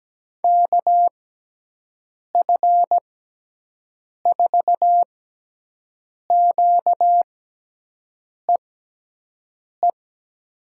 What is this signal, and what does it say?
Morse code, heard as a keyed tone, about 17 wpm, spaced as for 6 wpm, 700 Hz. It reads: KF4QEE
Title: USA Call Signs - 6 Characters - 6x Character Spacing - 17wpm